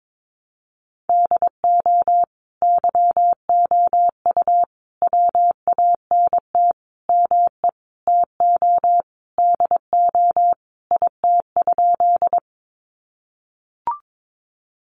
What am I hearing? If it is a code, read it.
Morse code, heard as a keyed tone, about 22 wpm, 700 Hz. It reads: DO YOU WANT ME TO DO IT?
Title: Sentences from Top 200 Words 22wpm